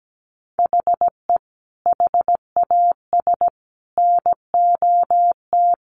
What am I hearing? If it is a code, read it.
Morse code, heard as a keyed tone, about 17 wpm, 700 Hz. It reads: HE HAS NOT
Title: Sentences from Top 1100 Words 17wpm